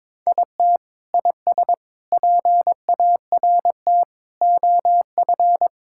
Morse code, heard as a keyed tone, about 21 wpm, 700 Hz. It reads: IT IS PART OF
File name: practice / 22wpm / Sentences with 4-Word Phrases - 22wpm